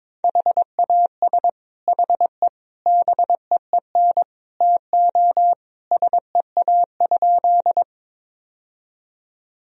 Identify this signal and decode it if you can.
Morse code, heard as a keyed tone, about 22 wpm, 700 Hz. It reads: HAS HE BEEN TO SEA?
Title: Sentences from Top 1000 Words - Review Part 1 - 22wpm